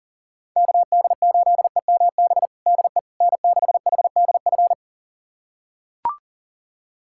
Morse code, heard as a keyed tone, about 40 wpm, 700 Hz. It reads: KD8EMB DE N6HDF
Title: USA Callsign DE Callsign 40wpm